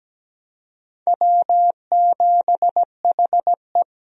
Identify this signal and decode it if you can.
Morse code, heard as a keyed tone, about 17 wpm, 700 Hz. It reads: W7HE
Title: USA Call Signs - 4 Characters - 17wpm